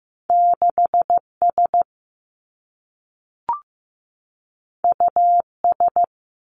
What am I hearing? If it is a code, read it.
Morse code, heard as a keyed tone, about 15 wpm, 700 Hz. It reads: 6 S U S
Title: Two Challenging Letter-Number Combinations 15wpm